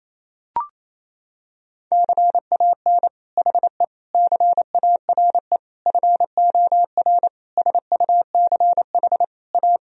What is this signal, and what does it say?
Morse code, heard as a keyed tone, about 28 wpm, 700 Hz. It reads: CAN HE CARE FOR SUCH A
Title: Sentences from Top 1000 Words - Review Part 1 - 28wpm